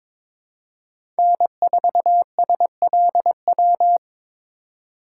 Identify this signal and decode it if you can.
Morse code, heard as a keyed tone, about 22 wpm, 700 Hz. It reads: N4SLW
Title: USA Call Signs - 2x Character Spacing - 22wpm